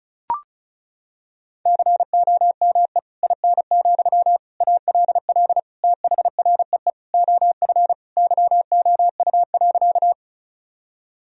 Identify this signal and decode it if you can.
Morse code, heard as a keyed tone, about 36 wpm, 700 Hz. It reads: COME IN, ALL THREE OF YOU.
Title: Sentences from Top 1000 Words - Review Part 1 - 35wpm